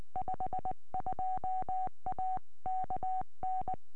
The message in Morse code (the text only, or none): 52A KN